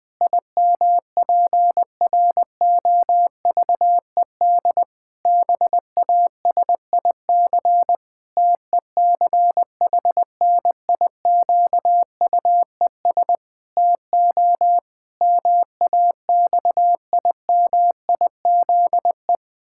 Morse code, heard as a keyed tone, about 20 wpm, 700 Hz. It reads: IMPROVED BASIC TECHNIQUES TO MAXIMIZE
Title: Pangram Challenge v2 - 20wpm